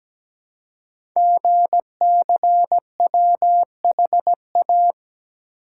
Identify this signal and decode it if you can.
Morse code, heard as a keyed tone, about 17 wpm, 700 Hz. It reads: GCWHA
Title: Code Groups - 5 Letter:Numbers - 17wpm